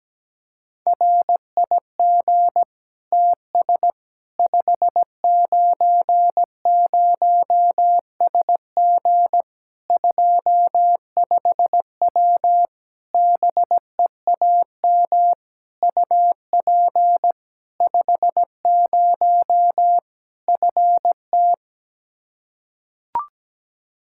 Morse code, heard as a keyed tone, about 17 wpm, 700 Hz. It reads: RIG TS 590SG 25W BEAM UP 50 FT
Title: Practice QSO - Exchange 2 - Info - 17wpm